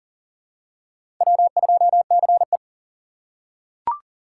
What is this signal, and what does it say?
Morse code, heard as a keyed tone, about 40 wpm, 700 Hz. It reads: W2CE
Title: USA Call Signs - 4 Characters - 6x Character Spacing - 40wpm